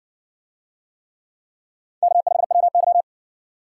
Morse code, heard as a keyed tone, about 57 wpm, 700 Hz. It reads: K5WY